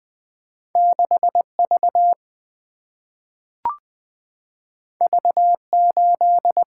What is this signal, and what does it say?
Morse code, heard as a keyed tone, about 20 wpm, 700 Hz. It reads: 6V V8
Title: Single-Letter Single-Number 20wpm